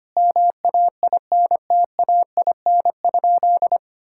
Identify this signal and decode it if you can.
Morse code, heard as a keyed tone, about 25 wpm, 700 Hz. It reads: MAINTAIN?
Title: Sentences from Top 1000 Words - Review Part 5 - 25wpm